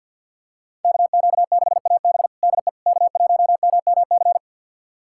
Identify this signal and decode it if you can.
Morse code, heard as a keyed tone, about 47 wpm, 700 Hz. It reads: KQ6AB DE K1MMC
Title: USA Callsign DE Callsign - 5 Characters - 50wpm